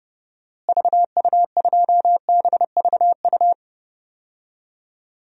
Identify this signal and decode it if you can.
Morse code, heard as a keyed tone, about 30 wpm, 700 Hz. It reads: VU2BVU
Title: DXCC-100 Call Signs - 6 Characters - Repeated 3x - 30wpm